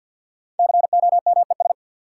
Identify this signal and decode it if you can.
Morse code, heard as a keyed tone, about 52 wpm, 700 Hz. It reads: COMES
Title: Sentences from Top 1000 Words - Review Part 5 - New Word Drill - 50wpm